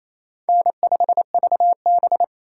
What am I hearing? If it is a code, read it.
Morse code, heard as a keyed tone, about 28 wpm, 700 Hz. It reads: N5VB
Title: USA Call Signs - 4 Characters - 4x Character Spacing - 28wpm